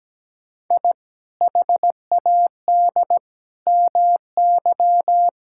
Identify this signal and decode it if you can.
Morse code, heard as a keyed tone, about 17 wpm, 700 Hz. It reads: I HAD MY
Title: Sentences from Top 1000 Words - Review Part 6 - 17wpm